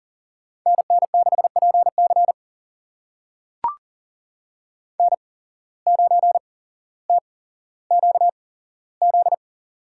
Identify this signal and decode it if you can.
Morse code, heard as a keyed tone, about 40 wpm, spaced as for 17 wpm, 700 Hz. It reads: NN6PC N9TQZ
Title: USA Call Signs - 5 Characters - 8x Character Spacing - 40wpm